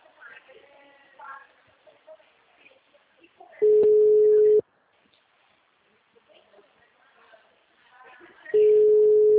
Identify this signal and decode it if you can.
Ringback tone (Europe, 425 Hz), with speech.
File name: out- -2009- - - .283